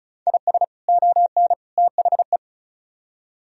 Morse code, heard as a keyed tone, about 43 wpm, 700 Hz. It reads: IS ON THE
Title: Speed-Racing - Sets of 3 Words 35wpm